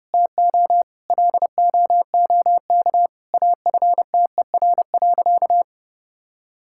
Morse code, heard as a keyed tone, about 30 wpm, 700 Hz. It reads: TO LOOK AFTER.
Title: Sentences from Top 1600 Words 30wpm